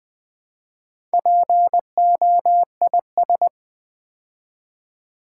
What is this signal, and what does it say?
Morse code, heard as a keyed tone, about 20 wpm, 700 Hz. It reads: POIS
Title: Build A Word - 2 Letters At A Time - 6-Letter Words - 20wpm